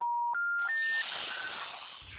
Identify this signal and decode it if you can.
Special information tone, with noise.